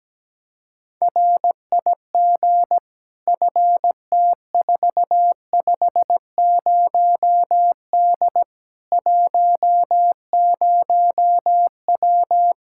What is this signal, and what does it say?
Morse code, heard as a keyed tone, about 17 wpm, 700 Hz. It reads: RIG FT450D 10W
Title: K1USN SST QSO Practice 17wpm